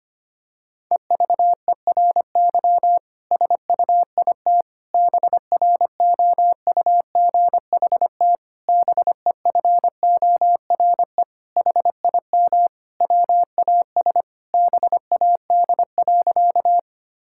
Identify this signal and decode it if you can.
Morse code, heard as a keyed tone, about 25 wpm, 700 Hz. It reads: EVERY SUIT BROUGHT BEFORE HIM WAS BAD.
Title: Sentences from Top 1600 Words 25wpm